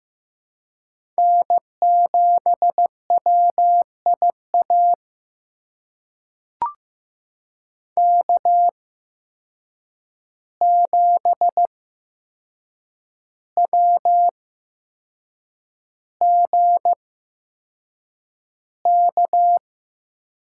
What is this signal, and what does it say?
Morse code, heard as a keyed tone, about 15 wpm, spaced as for 6 wpm, 700 Hz. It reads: N7WIA K7WGK
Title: USA Call Signs - 5 Characters - 8x Character Spacing - 15wpm